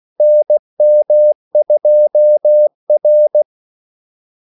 Morse code, heard as a keyed tone, about 16 wpm, 600 Hz. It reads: NM2R
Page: Kaldesignaler 16 wpm